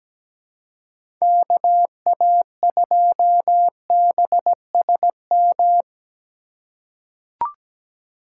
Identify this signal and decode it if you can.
Morse code, heard as a keyed tone, about 17 wpm, 700 Hz. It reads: KA2BSM
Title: USA Call Signs - 2x Character Spacing - 17wpm